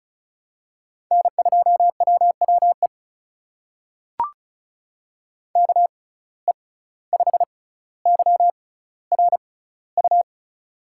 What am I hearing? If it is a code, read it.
Morse code, heard as a keyed tone, about 36 wpm, spaced as for 17 wpm, 700 Hz. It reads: N2WWE KE5YRU
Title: USA Call Signs - 6x Character Spacing - 35wpm